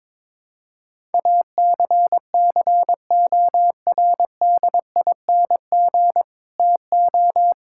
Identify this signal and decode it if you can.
Morse code, heard as a keyed tone, about 21 wpm, 700 Hz. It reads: ACCORDING TO
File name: practice / 22wpm / Sentences with 2-Word Phrases - 22wpm